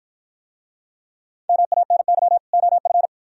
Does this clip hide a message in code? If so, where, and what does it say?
Yes: Morse code from 1.5 to 3.1 s, heard as a keyed tone, about 54 wpm, 700 Hz. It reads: MANY OF